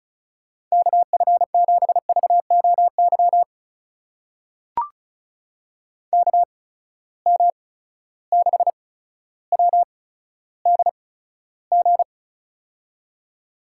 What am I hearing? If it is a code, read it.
Morse code, heard as a keyed tone, about 36 wpm, spaced as for 15 wpm, 700 Hz. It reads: KF7VOY KM6WDG
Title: USA Call Signs - 6 Characters - 8x Character Spacing - 35wpm